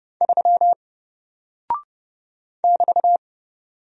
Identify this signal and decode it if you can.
Morse code, heard as a keyed tone, about 30 wpm, 700 Hz. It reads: 3 <BT>
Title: Single Character (Letter:Number:Punctuation:Prosign) Alt 30wpm